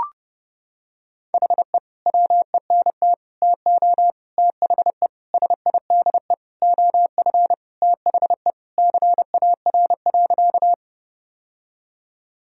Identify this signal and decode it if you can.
Morse code, heard as a keyed tone, about 30 wpm, 700 Hz. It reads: HE WENT TO THE SIDE OF THE CAR.
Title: Sentences from Top 1000 Words - Review Part 1 - 30wpm